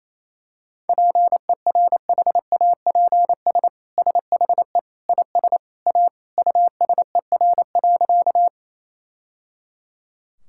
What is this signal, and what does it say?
Morse code, heard as a keyed tone, about 28 wpm, 700 Hz. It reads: PERHAPS SHE IS A USER.
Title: Sentences from Top 1000 Words - Review Part 5 - 28wpm